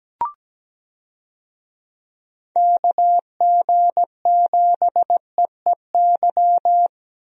Morse code, heard as a keyed tone, about 17 wpm, 700 Hz. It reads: KG7EEY
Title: USA Call Signs - 6 Characters - 17wpm